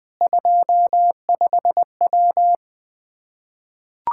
Morse code, heard as a keyed tone, about 20 wpm, 700 Hz. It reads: 25W
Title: Practice QSO - Exchange 2 - Info - 20wpm